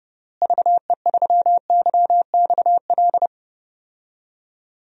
Morse code, heard as a keyed tone, about 30 wpm, 700 Hz. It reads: VE3YXL
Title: DXCC-100 Call Signs - 6 Characters - Repeated 2x - 30wpm